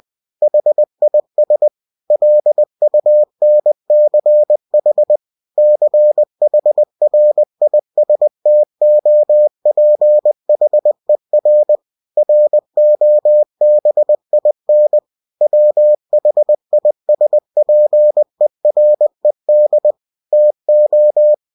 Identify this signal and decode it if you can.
Morse code, heard as a keyed tone, about 20 wpm, 600 Hz. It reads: HIS LUNCH CHRISTOPHER ROBIN WHISPERED TO
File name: cw-books / winnie-the-pooh / winnie-20wpm-page-0084